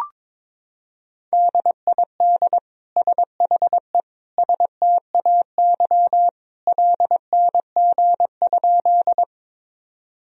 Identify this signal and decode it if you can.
Morse code, heard as a keyed tone, about 22 wpm, 700 Hz. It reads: DID SHE STAY LNG?
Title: Sentences from Top 500 Words and Abbreviations 22wpm